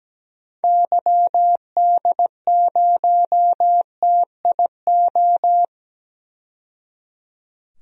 Morse code, heard as a keyed tone, about 17 wpm, 700 Hz. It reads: YD0TIO
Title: DXCC-100 Call Signs - 6 Characters - 17wpm